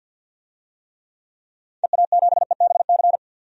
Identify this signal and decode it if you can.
Morse code, heard as a keyed tone, about 53 wpm, 700 Hz. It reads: EA7EBC
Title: DXCC-100 Call Signs - 6 Characters - 50wpm